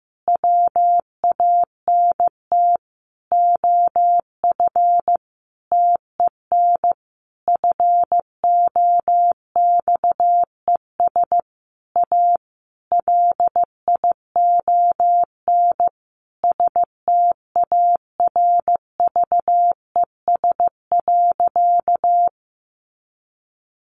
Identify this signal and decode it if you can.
Morse code, heard as a keyed tone, about 15 wpm, 700 Hz. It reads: WANT OF TEN FOXES A LION STARVES.
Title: A Sound Of Thunder 15wpm